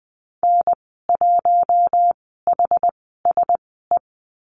Morse code, heard as a keyed tone, about 20 wpm, spaced as for 14 wpm, 700 Hz. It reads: N1HSE
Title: 062 Lesson 11g - Letter h - Call Signs 20wpm